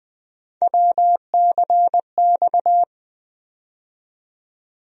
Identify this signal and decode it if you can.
Morse code, heard as a keyed tone, about 20 wpm, 700 Hz. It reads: WCX
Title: USA Call Sign Postfixes - 3 Letters 20wpm